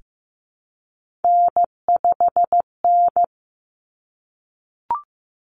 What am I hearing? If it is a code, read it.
Morse code, heard as a keyed tone, about 15 wpm, 700 Hz. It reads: N5N
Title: Common Cut Numbers - 3-Digit 15wpm